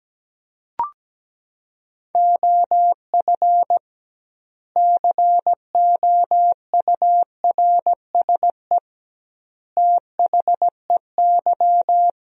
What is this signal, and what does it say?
Morse code, heard as a keyed tone, about 17 wpm, 700 Hz. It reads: OF COURSE THEY
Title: 2x Word-Spacing - Sentences from Top 500 Words - 17wpm